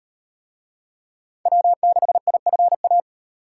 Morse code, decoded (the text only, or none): W6IFA